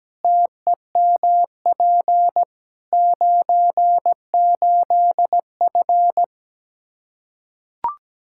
Morse code, heard as a keyed tone, about 17 wpm, 700 Hz. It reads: TEMP 98F
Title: K1USN SST QSO Practice 17wpm